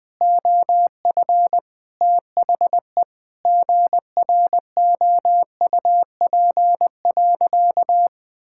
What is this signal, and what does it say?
Morse code, heard as a keyed tone, about 20 wpm, 700 Hz. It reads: OF THE GROUP.
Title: Sentences with 5-Word Phrases - 20wpm